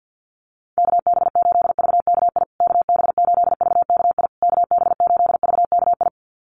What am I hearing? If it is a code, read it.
Morse code, heard as a keyed tone, about 48 wpm, 700 Hz. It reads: KB8VKI KB8VKI KB8VKI
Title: USA Call Signs Repeated 3x 50wpm